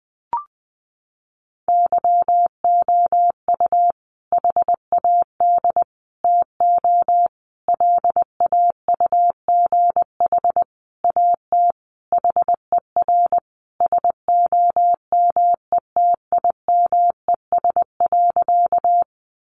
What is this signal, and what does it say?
Morse code, heard as a keyed tone, about 20 wpm, 700 Hz. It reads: YOU HAD TO LAUGH AT HER SOMETIMES.
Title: Sentences from Top 600 Words 20wpm